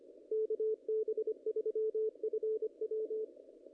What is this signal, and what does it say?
Morse code, heard as a keyed tone, about 24 wpm, 410 Hz. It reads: KB3FW
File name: contests / ss-cw-k5tr-op-n5rz -0100